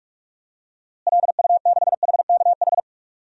Morse code, heard as a keyed tone, about 46 wpm, 700 Hz. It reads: LU6HKH
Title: DXCC-100 Call Signs - 6 Characters - Repeated 3x - 45wpm